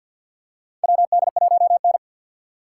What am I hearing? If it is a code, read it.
Morse code, heard as a keyed tone, about 50 wpm, 700 Hz. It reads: WD1N
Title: USA Call Signs - 4 Characters - 3x Character Spacing - 50wpm